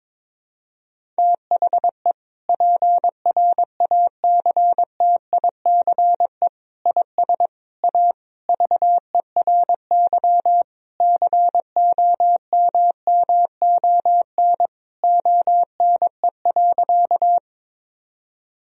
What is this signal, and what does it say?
Morse code, heard as a keyed tone, about 22 wpm, 700 Hz. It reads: THE PRACTICE IS A VERY COMMON ONE.
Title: Sentences from Top 1000 Words - Review Part 4 - 22wpm